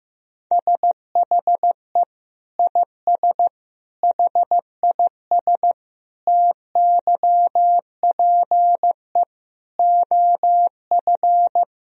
Morse code, heard as a keyed tone, about 15 wpm, 700 Hz. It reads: SHE IS HIS TYPE OF
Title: Sentences from Top 1000 Words - Review Part 3 - 15wpm